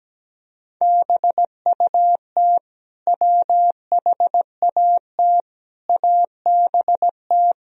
Morse code, heard as a keyed tone, about 17 wpm, 700 Hz. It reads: BUT WHAT ABT
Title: Sentences from Top 200 Words and Abbreviations 17wpm